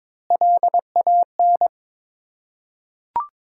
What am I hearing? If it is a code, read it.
Morse code, heard as a keyed tone, about 23 wpm, 700 Hz. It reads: LAN
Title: Ontario QSO Party Exchange 22wpm